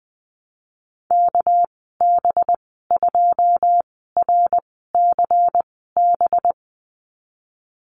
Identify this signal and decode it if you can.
Morse code, heard as a keyed tone, about 20 wpm, spaced as for 15 wpm, 700 Hz. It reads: KB2RCB
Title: 207 Lesson 33g - Letter j - Call Signs 20wpm